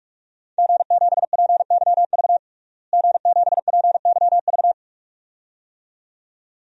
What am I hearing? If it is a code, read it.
Morse code, heard as a keyed tone, about 46 wpm, 700 Hz. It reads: G7PYV G7PYV
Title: DXCC-100 Call Signs - 5 Characters - Repeated 2x - 45wpm